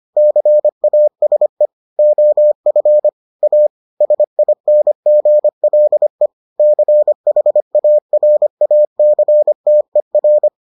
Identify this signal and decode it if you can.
Morse code, heard as a keyed tone, about 25 wpm, 600 Hz. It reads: CASE OF A SINGLE CHARACTER